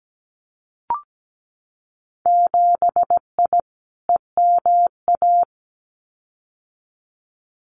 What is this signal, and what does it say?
Morse code, heard as a keyed tone, about 17 wpm, 700 Hz. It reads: 7I EMA
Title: Winter Field Day Exchanges Encore 17wpm